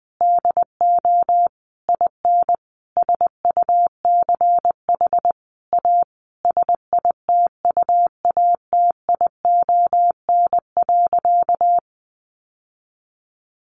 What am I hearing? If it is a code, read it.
Morse code, heard as a keyed tone, about 20 wpm, 700 Hz. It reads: DO IN SUCH A SITUATION.
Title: Sentences from Top 1200 Words 20wpm